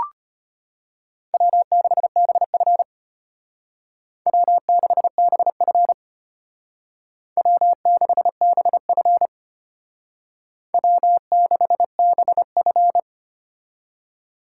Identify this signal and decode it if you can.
Morse code, heard as a keyed tone, about 32 wpm, 700 Hz. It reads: W6BF W6BF W6BF W6BF